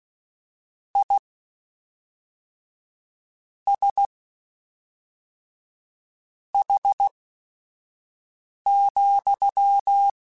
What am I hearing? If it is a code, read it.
Morse code, heard as a keyed tone, about 16 wpm, spaced as for 4 wpm, 800 Hz. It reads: ISH,
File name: General / General-mp3s / Suffixes sp @5